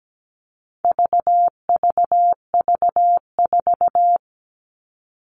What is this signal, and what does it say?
Morse code, heard as a keyed tone, about 17 wpm, 700 Hz. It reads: VVV4
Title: All Cut Numbers - 4-Digit 17wpm